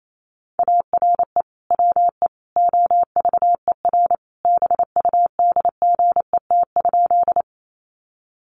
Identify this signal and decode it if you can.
Morse code, heard as a keyed tone, about 28 wpm, 700 Hz. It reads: ARE WE OVER BUDGET?
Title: Sentences from Top 1200 Words 28wpm